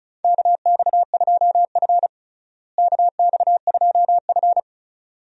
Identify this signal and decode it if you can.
Morse code, heard as a keyed tone, about 36 wpm, 700 Hz. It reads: KX2F KX2F
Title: USA Call Signs - 4 Characters - Repeated 2x - 35wpm